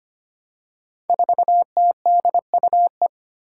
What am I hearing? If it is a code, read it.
Morse code, heard as a keyed tone, about 25 wpm, 700 Hz. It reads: 4TDUE
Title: Five Challenging Letter-Number Combinations 25wpm